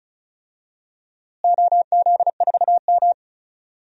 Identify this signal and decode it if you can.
Morse code, heard as a keyed tone, about 36 wpm, 700 Hz. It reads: OZ4M
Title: DXCC-100 Call Signs - 4 Characters - Repeated 2x - 35wpm